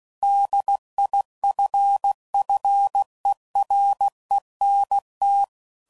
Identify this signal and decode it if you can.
Morse code, heard as a keyed tone, about 16 wpm, 800 Hz. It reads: DIFFERENT
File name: CW Code Groups @15WPM / Technical Subjects / PwrSupply